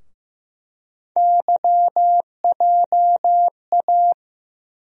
Morse code, heard as a keyed tone, about 15 wpm, 700 Hz. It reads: YJA
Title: USA Call Sign Postfixes - 3 Letters 15wpm